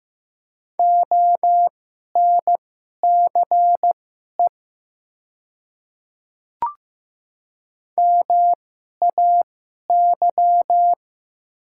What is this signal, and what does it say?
Morse code, heard as a keyed tone, about 15 wpm, spaced as for 11 wpm, 700 Hz. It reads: ONCE MAY
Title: 2x Character Spacing - Top 300 Words - 15wpm